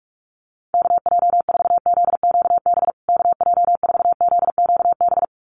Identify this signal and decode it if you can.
Morse code, heard as a keyed tone, about 44 wpm, 700 Hz. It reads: KJ4ZQB KJ4ZQB
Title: USA Call Signs Repeated 2x 45wpm